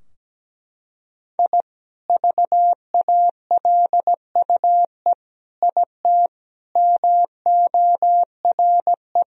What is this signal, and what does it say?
Morse code, heard as a keyed tone, about 17 wpm, 700 Hz. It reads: I VALUE IT MORE
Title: Sentences from Top 1000 Words - Review Part 3 - 17wpm